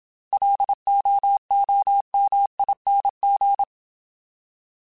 Morse code, heard as a keyed tone, about 27 wpm, 800 Hz. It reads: LOOMING